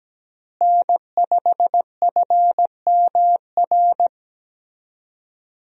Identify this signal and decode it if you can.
Morse code, heard as a keyed tone, about 17 wpm, 700 Hz. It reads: N5FMR
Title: USA Call Signs - 5 Characters - Repeated 3x - 17wpm